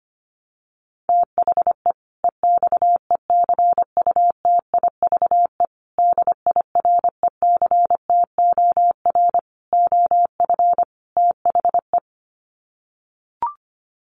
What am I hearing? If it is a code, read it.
Morse code, heard as a keyed tone, about 25 wpm, 700 Hz. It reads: THE EXECUTIVE DIRECTOR OF THE
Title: Sets of 5 Words Encore 25wpm